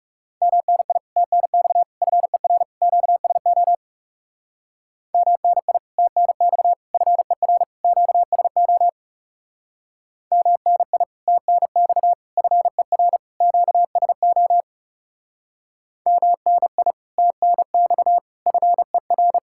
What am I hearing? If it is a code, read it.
Morse code, heard as a keyed tone, about 36 wpm, 700 Hz. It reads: MNI TNX FER QSO MNI TNX FER QSO MNI TNX FER QSO MNI TNX FER
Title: Speed-Racing Common QSO Phrases Encore - 30wpm